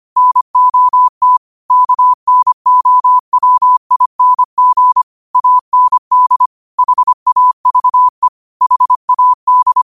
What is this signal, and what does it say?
Morse code, heard as a keyed tone, about 25 wpm, 1000 Hz. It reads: NOT KNOWING AND HAVE HAD
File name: qotdfiles / 25wpm